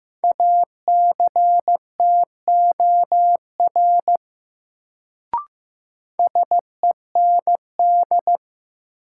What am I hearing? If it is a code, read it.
Morse code, heard as a keyed tone, about 15 wpm, 700 Hz. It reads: ACTOR SEND
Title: Sentences from Top 1500 Words - New Word Drill Alt - 15wpm